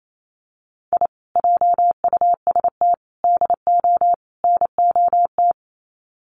Morse code, heard as a keyed tone, about 28 wpm, 700 Hz. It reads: I JUST DO NOT
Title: Sets of 4 Words 28wpm